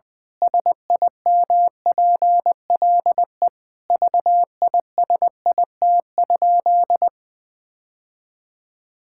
Morse code, heard as a keyed tone, about 20 wpm, 700 Hz. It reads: SIMPLE VISIT?
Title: Sentences from Top 1000 Words - Review Part 6 - 20wpm